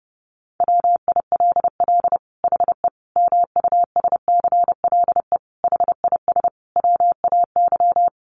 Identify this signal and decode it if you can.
Morse code, heard as a keyed tone, about 30 wpm, 700 Hz. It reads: WILL HE MUSCLE HIS WAY
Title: Sentences from Top 1300 Words 30wpm